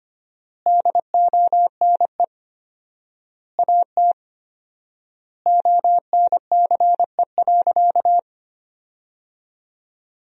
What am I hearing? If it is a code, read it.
Morse code, heard as a keyed tone, about 25 wpm, 700 Hz. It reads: DONE AT ONCE.
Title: 4x Word-Spacing - Sentences from Top 500 Words - 25wpm